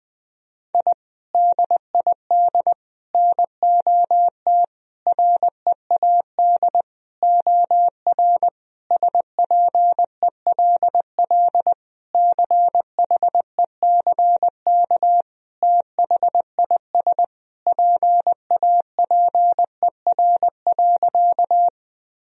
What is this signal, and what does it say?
Morse code, heard as a keyed tone, about 20 wpm, 700 Hz. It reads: I DID NOT READ OR SPELL CHECK THIS PAPER.
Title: Sentences from Top 1000 Words - Review Part 3 - 20wpm